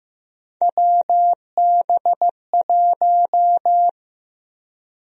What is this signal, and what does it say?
Morse code, heard as a keyed tone, about 15 wpm, 700 Hz. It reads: WB1
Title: USA Call Sign Prefixes - Repeated 2x - 15wpm